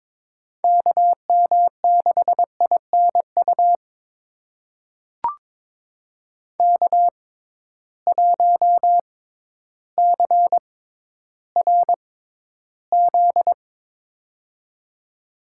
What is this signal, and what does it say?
Morse code, heard as a keyed tone, about 22 wpm, spaced as for 12 wpm, 700 Hz. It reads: KM6INU K1CRZ
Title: USA Call Signs - 6x Character Spacing - 22wpm